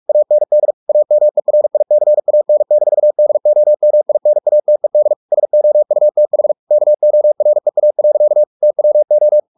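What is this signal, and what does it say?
Morse code, heard as a keyed tone, about 44 wpm, 600 Hz. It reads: AND AMERICAN-DOMINATED SOUTH KOREA. TWO
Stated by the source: Morse practice at 45 wpm, the novel Never by Ken Follett in English, from CW med Gnister.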